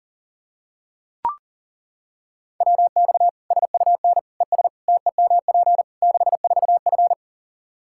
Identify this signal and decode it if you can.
Morse code, heard as a keyed tone, about 40 wpm, 700 Hz. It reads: WX SUN ES TEMP 64F
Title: Practice QSO - Exchange 2 - Info - 40wpm